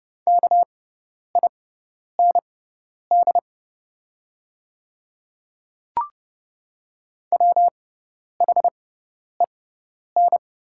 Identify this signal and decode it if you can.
Morse code, heard as a keyed tone, about 30 wpm, spaced as for 10 wpm, 700 Hz. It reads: KIND WHEN
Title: 6x Character Spacing - Top 200 Words - 30wpm